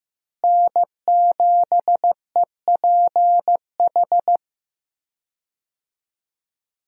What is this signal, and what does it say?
Morse code, heard as a keyed tone, about 15 wpm, 700 Hz. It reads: N7EPH
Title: USA Call Signs - 5 Characters - Repeated 2x - 15wpm